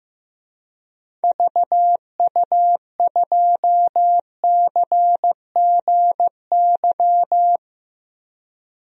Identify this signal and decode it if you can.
Morse code, heard as a keyed tone, about 15 wpm, 700 Hz. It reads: VU2CGY
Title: DXCC-100 Call Signs - 6 Characters - 15wpm